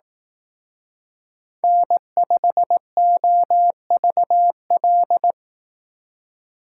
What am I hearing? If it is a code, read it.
Morse code, heard as a keyed tone, about 18 wpm, 700 Hz. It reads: N5OVL
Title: Speed Racing - USA Call Signs 15wpm